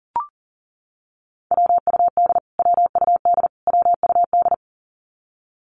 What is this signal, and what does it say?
Morse code, heard as a keyed tone, about 40 wpm, 700 Hz. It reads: WUD WUD WUD
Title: Abbreviations Repeated 3x - 40wpm